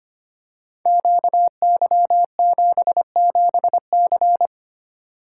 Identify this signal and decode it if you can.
Morse code, heard as a keyed tone, about 25 wpm, 700 Hz. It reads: QY77C
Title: Code Groups - 5 Letter:Numbers - 25wpm